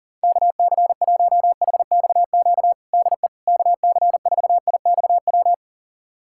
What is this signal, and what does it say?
Morse code, heard as a keyed tone, about 37 wpm, 700 Hz. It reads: KC1HXQ DE KC4IXW
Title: USA Callsign DE Callsign - 6 Characters - 40wpm